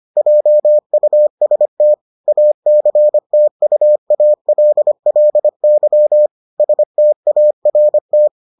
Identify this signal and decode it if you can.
Morse code, heard as a keyed tone, about 25 wpm, 600 Hz. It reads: JUST ACTUALLY START